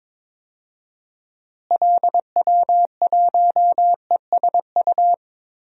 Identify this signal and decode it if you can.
Morse code, heard as a keyed tone, about 22 wpm, 700 Hz. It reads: LW1ESU
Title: DXCC-100 Call Signs - 6 Characters - Repeated 3x - 22wpm